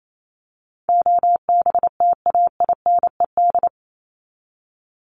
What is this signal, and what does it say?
Morse code, heard as a keyed tone, about 28 wpm, 700 Hz. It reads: OBTAINED
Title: ED Suffix - Top 100 Words with Base Words 28wpm